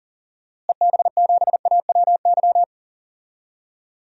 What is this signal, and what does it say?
Morse code, heard as a keyed tone, about 39 wpm, 700 Hz. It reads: EB7AWY
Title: DXCC-100 Call Signs - 6 Characters - Repeated 2x - 40wpm